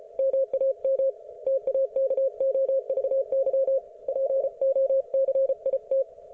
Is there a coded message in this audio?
Yes: Morse code from 0.2 to 6.0 s, heard as a keyed tone, about 33 wpm, 530 Hz. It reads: MAM TAKOVY POCIT